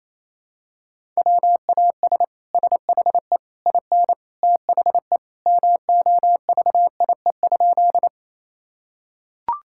Morse code, heard as a keyed tone, about 28 wpm, 700 Hz. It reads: WAS SHE IN THE MOVIE?
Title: Sentences from Top 1000 Words - Review Part 5 - 28wpm